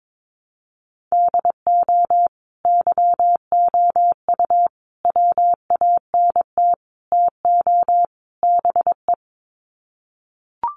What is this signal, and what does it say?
Morse code, heard as a keyed tone, about 22 wpm, 700 Hz. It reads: DO YOU WANT TO BE
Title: Sets of 5 Words Encore2 22wpm